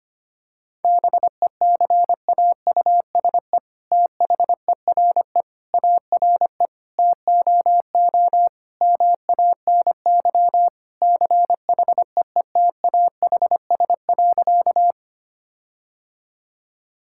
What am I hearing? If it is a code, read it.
Morse code, heard as a keyed tone, about 25 wpm, 700 Hz. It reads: BECAUSE THERE ARE TOO MANY CHEETAHS.
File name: practice / 25wpm / All-Age Appropriate Jokes - 25wpm